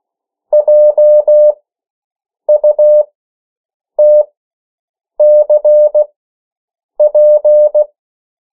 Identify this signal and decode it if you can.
Morse code, heard as a keyed tone, about 16 wpm, spaced as for 7 wpm, 600 Hz. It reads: JUTCP
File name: morse mp3 / 14 urok 13